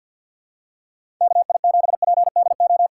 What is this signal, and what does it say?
Morse code, heard as a keyed tone, about 46 wpm, 700 Hz. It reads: KI7PDO
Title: USA Callsign DE Callsign - 6 Characters - 50wpm